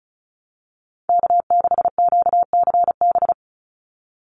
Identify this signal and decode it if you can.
Morse code, heard as a keyed tone, about 34 wpm, 700 Hz. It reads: K6QCB
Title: NA SOTA Chaser Callsigns 35wpm